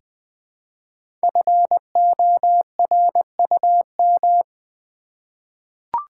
Morse code, heard as a keyed tone, about 20 wpm, 700 Hz. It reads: FORUM
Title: 5 Letter Words - First 3 Letters Given - 20wpm